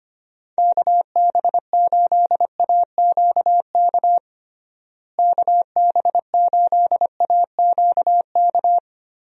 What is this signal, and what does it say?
Morse code, heard as a keyed tone, about 25 wpm, 700 Hz. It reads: KB8AQK KB8AQK
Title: USA Call Signs - 6 Characters - Repeated 3x - 25wpm